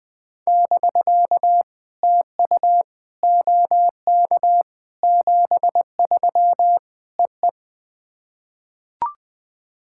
Morse code, heard as a keyed tone, about 20 wpm, 700 Hz. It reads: <BK> TU OK 73 EE
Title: POTA QSO Practice - 20wpm